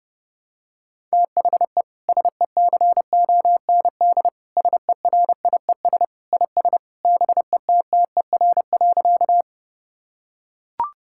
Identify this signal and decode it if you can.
Morse code, heard as a keyed tone, about 31 wpm, 700 Hz. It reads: THE SECOND SERIES IS BETTER.
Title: Sentences from Top 1000 Words - Review Part 5 - 30wpm